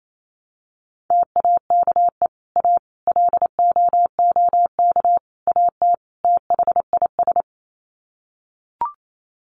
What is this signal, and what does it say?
Morse code, heard as a keyed tone, about 28 wpm, 700 Hz. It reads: TAKE A LOOK AT THIS
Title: Sets of 5 Words 28wpm